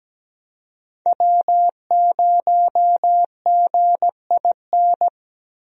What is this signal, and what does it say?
Morse code, heard as a keyed tone, about 17 wpm, 700 Hz. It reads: W0GIN
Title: USA Call Signs - 5 Characters - Repeated 2x - 17wpm